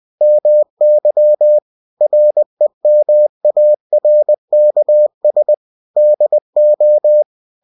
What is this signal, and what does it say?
Morse code, heard as a keyed tone, about 20 wpm, 600 Hz. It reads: MY REMARKS DO
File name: cw-books / winnie-the-pooh / winnie-20wpm-page-0107